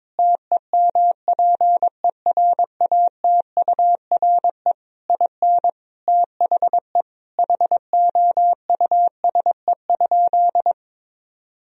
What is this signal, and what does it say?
Morse code, heard as a keyed tone, about 22 wpm, 700 Hz. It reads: TEMPERATURE IN THE HOUSE?
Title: Sentences from Top 1000 Words - Review Part 4 - 22wpm